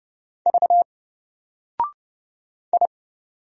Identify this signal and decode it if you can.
Morse code, heard as a keyed tone, about 30 wpm, 700 Hz. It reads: V I
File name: Single-Letter-Number-30wpm